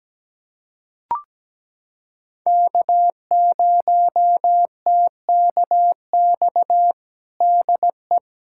Morse code, heard as a keyed tone, about 17 wpm, 700 Hz. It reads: K0TKX DE
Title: USA Callsign DE Callsign - 5 Characters - 17wpm